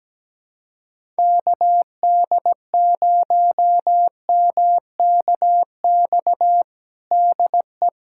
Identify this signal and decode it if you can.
Morse code, heard as a keyed tone, about 17 wpm, 700 Hz. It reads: KD0MKX DE
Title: USA Callsign DE Callsign - 6 Characters - 17wpm